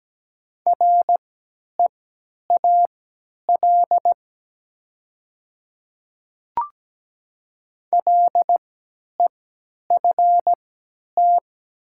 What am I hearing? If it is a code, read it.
Morse code, heard as a keyed tone, about 17 wpm, spaced as for 10 wpm, 700 Hz. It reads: REAL LEFT
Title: 3x Character Spacing - Top 300 Words - 17wpm